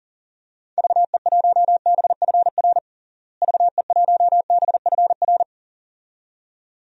Morse code, heard as a keyed tone, about 40 wpm, 700 Hz. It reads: VE1BFR VE1BFR
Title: DXCC-100 Call Signs - 6 Characters - Repeated 2x - 40wpm